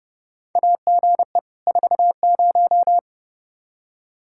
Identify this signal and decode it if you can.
Morse code, heard as a keyed tone, about 30 wpm, 700 Hz. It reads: AGE 40
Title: Practice QSO - Exchange 3 - More Info - 30wpm